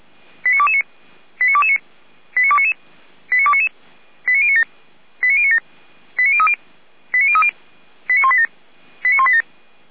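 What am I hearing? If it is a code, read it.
Five-tone selective call (ZVEI), heard as quick runs of five tones, tones of about 70 ms. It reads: 89229 89229 89299 89299 89097 89097 89030 89030 89187 89187
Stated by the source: ZVEI five-tone selective call received by SDR, from the fire brigade server feuerwehr-orlamuende.org.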